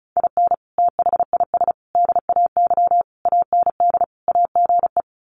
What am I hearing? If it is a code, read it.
Morse code, heard as a keyed tone, about 34 wpm, 700 Hz. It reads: IN THIS DAY AND AGE
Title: Sets of 5 Words Encore 35wpm